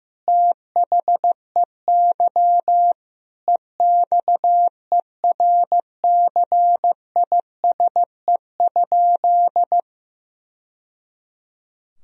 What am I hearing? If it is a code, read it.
Morse code, heard as a keyed tone, about 15 wpm, 700 Hz. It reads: THEY EXERCISE?
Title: Sentences from Top 1000 Words - Review Part 4 - 15wpm